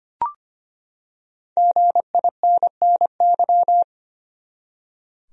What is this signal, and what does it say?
Morse code, heard as a keyed tone, about 25 wpm, 700 Hz. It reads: GINNY
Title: USA Common Names - Max 5 Letters - 25wpm